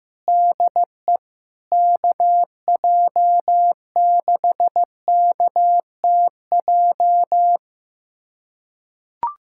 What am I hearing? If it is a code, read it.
Morse code, heard as a keyed tone, about 15 wpm, 700 Hz. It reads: DE KJ6KTJ
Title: USA Callsign DE Callsign 15wpm